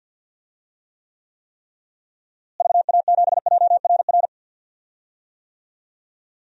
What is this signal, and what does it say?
Morse code, heard as a keyed tone, about 52 wpm, 700 Hz. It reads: VA7JRR
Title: DXCC-100 Call Signs - 6 Characters - Repeated 3x - 50wpm